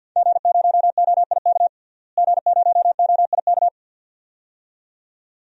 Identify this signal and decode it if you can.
Morse code, heard as a keyed tone, about 46 wpm, 700 Hz. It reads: G0OIK G0OIK
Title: DXCC-100 Call Signs - 5 Characters - Repeated 2x - 50wpm